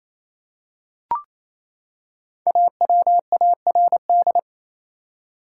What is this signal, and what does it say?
Morse code, heard as a keyed tone, about 28 wpm, 700 Hz. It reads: AWARD
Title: Five Letter Words - 28wpm